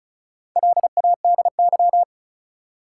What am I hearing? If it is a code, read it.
Morse code, heard as a keyed tone, about 34 wpm, 700 Hz. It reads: LADY
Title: Four Letter Words - 35wpm